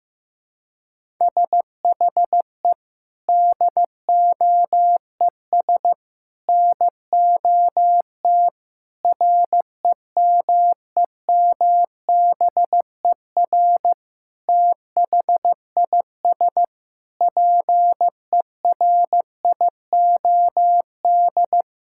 Morse code, heard as a keyed tone, about 15 wpm, 700 Hz. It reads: SHE DOES NOT REMEMBER THIS PERIOD
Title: Sentences from Top 1000 Words - Review Part 5 - 15wpm